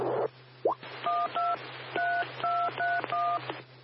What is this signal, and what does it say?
DTMF, heard as telephone touch tones, tones of about 240 ms. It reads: 12 3231
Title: W6NUT 147.450 MHz Repeater